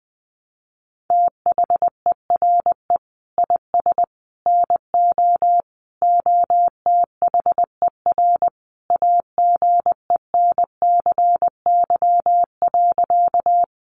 Morse code, heard as a keyed tone, about 20 wpm, 700 Hz. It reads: THERE IS NO OTHER AGENCY.
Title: Sentences from Top 1300 Words 20wpm